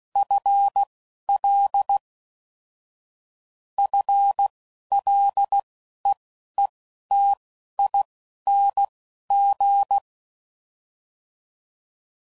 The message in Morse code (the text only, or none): FL FLEETING